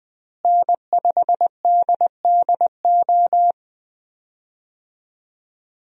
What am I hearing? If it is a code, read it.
Morse code, heard as a keyed tone, about 21 wpm, 700 Hz. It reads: N5DDO
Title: USA Call Signs - 5 Characters - Repeated 3x - 20wpm